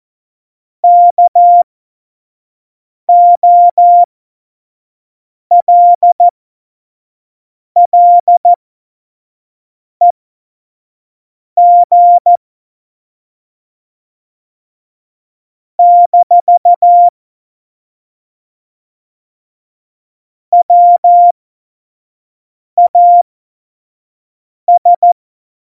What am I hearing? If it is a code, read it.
Morse code, heard as a keyed tone, about 14 wpm, spaced as for 5 wpm, 700 Hz. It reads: KOLLEG - WAS
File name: Morsekurs / Witze / Witz 07. Januar 2020, 25 F70